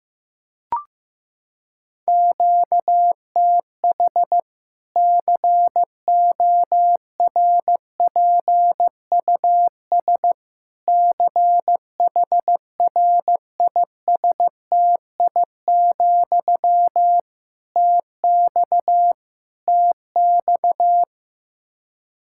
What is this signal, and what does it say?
Morse code, heard as a keyed tone, about 15 wpm, 700 Hz. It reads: QTH CORPUS CHRISTI, TX TX
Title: Practice QSO - Exchange 1 - Beginning - 15wpm